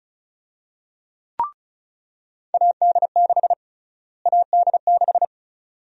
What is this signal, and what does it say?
Morse code, heard as a keyed tone, about 36 wpm, 700 Hz. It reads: AD6 AD6
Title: USA Call Sign Prefixes - Repeated 2x - 35wpm